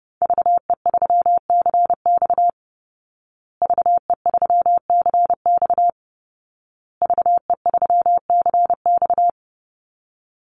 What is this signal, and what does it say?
Morse code, heard as a keyed tone, about 30 wpm, 700 Hz. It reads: VE3CX VE3CX VE3CX
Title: Call Signs Repeated 2x - 30wpm